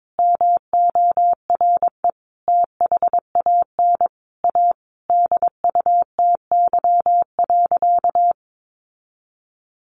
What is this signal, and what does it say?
Morse code, heard as a keyed tone, about 22 wpm, 700 Hz. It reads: MORE THAN A DUTY.
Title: Sentences from Top 1500 Words 22wpm